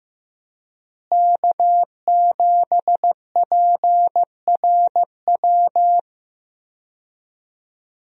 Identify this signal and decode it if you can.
Morse code, heard as a keyed tone, about 15 wpm, 700 Hz. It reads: K7PRW
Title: USA Call Signs - 5 Characters - Repeated 2x - 15wpm